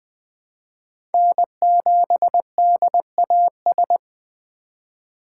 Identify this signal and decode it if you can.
Morse code, heard as a keyed tone, about 20 wpm, 700 Hz. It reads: N7DAS
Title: USA Call Signs - 5 Characters - 2x Character Spacing - 20wpm